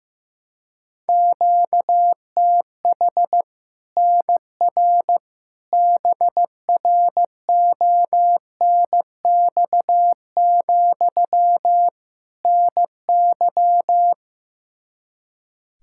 Morse code, heard as a keyed tone, about 15 wpm, 700 Hz. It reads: QTH NR BRONX, NY
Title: Practice QSO - Exchange 1 - Beginning - 15wpm